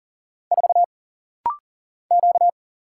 Morse code, heard as a keyed tone, about 40 wpm, 700 Hz. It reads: 4 Q
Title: Single Letter-Number Alt - Rapid-Fire 40wpm